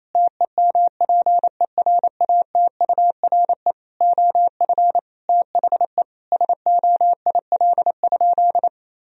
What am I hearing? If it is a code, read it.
Morse code, heard as a keyed tone, about 28 wpm, 700 Hz. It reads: TEMPERATURE OF THE SOIL?
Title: Sentences from Top 1000 Words - Review Part 4 - 28wpm